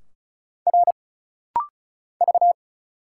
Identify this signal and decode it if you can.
Morse code, heard as a keyed tone, about 35 wpm, 700 Hz. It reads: R V
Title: Single Letters - Rapid-Fire 35wpm